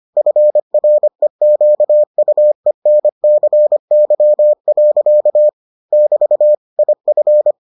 Morse code, heard as a keyed tone, about 25 wpm, 600 Hz. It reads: FREQUENCY. <BT> IF